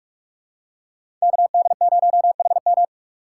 Morse code, heard as a keyed tone, about 44 wpm, 690 Hz. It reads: KD0HM
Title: USA Call Signs - 5 Characters - 6x Character Spacing - 45wpm